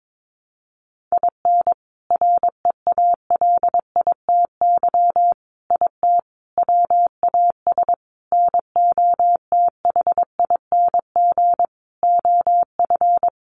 Morse code, heard as a keyed tone, about 22 wpm, 700 Hz. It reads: IN REALITY IT WAS NOTHING OF
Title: Sentences from Top 1200 Words 22wpm